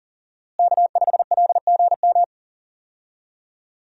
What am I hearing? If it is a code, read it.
Morse code, heard as a keyed tone, about 40 wpm, 700 Hz. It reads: K5LGM
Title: USA Call Signs - 5 Characters - Repeated 2x - 40wpm